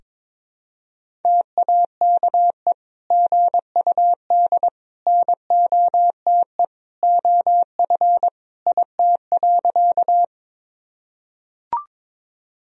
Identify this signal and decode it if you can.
Morse code, heard as a keyed tone, about 22 wpm, 700 Hz. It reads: TAKE GUD NOTE OF IT.
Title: Sentences from Top 500 Words and Abbreviations 22wpm